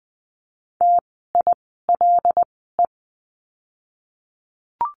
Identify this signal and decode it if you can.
Morse code, heard as a keyed tone, about 20 wpm, spaced as for 15 wpm, 700 Hz. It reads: TILE
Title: 075 Lesson 13f - Letter L - Words 20wpm